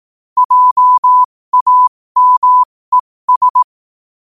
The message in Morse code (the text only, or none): JAMES